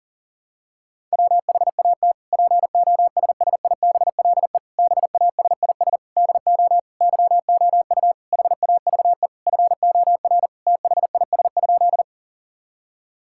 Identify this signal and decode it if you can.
Morse code, heard as a keyed tone, about 37 wpm, 700 Hz. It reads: WHAT POSSIBLE BASIS DO YOU HAVE FOR THIS?
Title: Sentences from Top 1000 Words - Review Part 5 - 40wpm